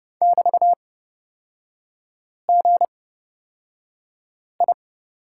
Morse code, heard as a keyed tone, about 30 wpm, spaced as for 5 wpm, 700 Hz. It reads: <BT>GI
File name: Lesson 36e - BT - Three Characters 30wpm